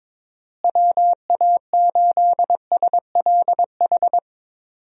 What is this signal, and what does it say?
Morse code, heard as a keyed tone, about 22 wpm, 700 Hz. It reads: WA8SLH
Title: USA Call Signs - 6 Characters - Repeated 3x - 22wpm